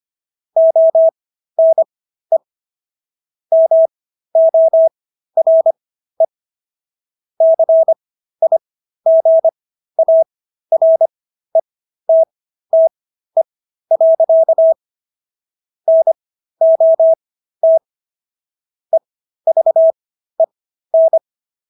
Morse code, heard as a keyed tone, about 25 wpm, spaced as for 13 wpm, 650 Hz. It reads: ONE MORE CIGARETTE. NOT EVEN